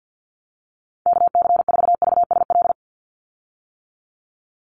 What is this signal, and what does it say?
Morse code, heard as a keyed tone, about 50 wpm, 700 Hz. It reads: KC4VSL
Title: USA Call Signs 50wpm